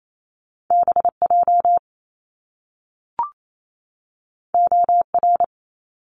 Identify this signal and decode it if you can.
Morse code, heard as a keyed tone, about 28 wpm, 700 Hz. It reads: B J O R
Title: Top Two Letter Combinations in Words 28wpm